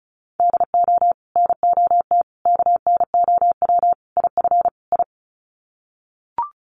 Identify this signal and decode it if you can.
Morse code, heard as a keyed tone, about 34 wpm, 700 Hz. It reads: DO NOT KNOW IF I
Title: Sets of 5 Words 35wpm